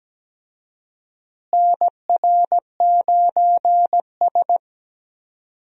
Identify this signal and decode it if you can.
Morse code, heard as a keyed tone, about 17 wpm, 700 Hz. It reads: NR9S
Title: USA Call Signs - 4 Characters - Repeated 2x - 17wpm